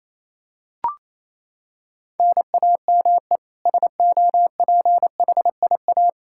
Morse code, heard as a keyed tone, about 28 wpm, 700 Hz. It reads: NAME SOPHIA
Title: Practice QSO - Exchange 1 - Beginning - 28wpm